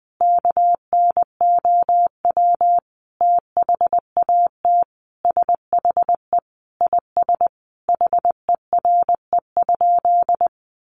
Morse code, heard as a keyed tone, about 20 wpm, 700 Hz. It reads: KNOW THAT SHE IS HERE?
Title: Sentences from Top 1300 Words 20wpm